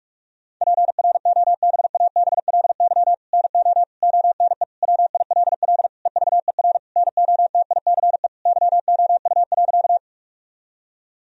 Morse code, heard as a keyed tone, about 46 wpm, 700 Hz. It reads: PROBABLY NO ONE WILL EVER NOTICE YOU.
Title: Sentences from Top 1000 Words - Review Part 4 - 45wpm